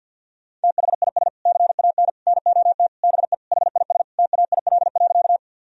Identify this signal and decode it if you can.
Morse code, heard as a keyed tone, about 47 wpm, 700 Hz. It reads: THIS CAN NOT BE HIS TAIL.
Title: Sentences from Top 1000 Words - Review Part 3 - 50wpm